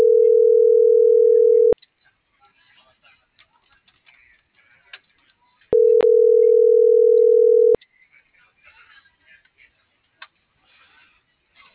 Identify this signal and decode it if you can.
Ringback tone (North America), 440 + 480 Hz.